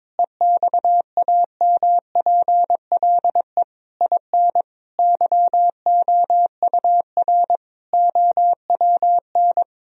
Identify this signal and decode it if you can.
Morse code, heard as a keyed tone, about 22 wpm, 700 Hz. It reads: EXAMPLE IN YOUR OWN
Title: Sentences from Top 1000 Words - Review Part 1 - 22wpm